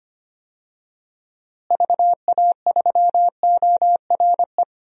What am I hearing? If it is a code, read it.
Morse code, heard as a keyed tone, about 25 wpm, 700 Hz. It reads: VA3ORE
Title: DXCC-100 Call Signs - 6 Characters - Repeated 2x - 25wpm